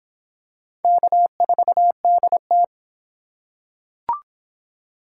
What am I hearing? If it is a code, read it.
Morse code, heard as a keyed tone, about 25 wpm, 700 Hz. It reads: K4DT